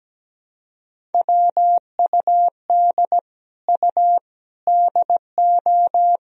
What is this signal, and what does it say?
Morse code, heard as a keyed tone, about 17 wpm, 700 Hz. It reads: WUD U DO
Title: Sentences from Top 100 Words and Abbreviations 17wpm